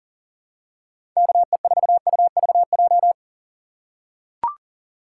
Morse code, heard as a keyed tone, about 40 wpm, 700 Hz. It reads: KE4UVJ
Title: USA Call Signs - 6 Characters - 6x Character Spacing - 40wpm